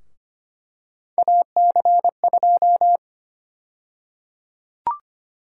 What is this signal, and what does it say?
Morse code, heard as a keyed tone, about 25 wpm, 700 Hz. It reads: AC2
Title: USA Call Sign Prefixes - 25wpm